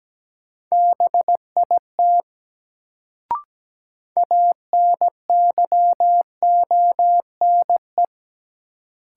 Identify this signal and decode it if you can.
Morse code, heard as a keyed tone, about 17 wpm, 700 Hz. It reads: BIT ANYONE
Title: Sentences from Top 900 Words - New Word Drill - 17wpm